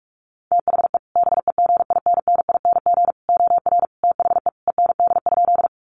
Morse code, heard as a keyed tone, about 44 wpm, 700 Hz. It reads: THE BEGINNING OR THE END?
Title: Sentences from Top 800 Words 45wpm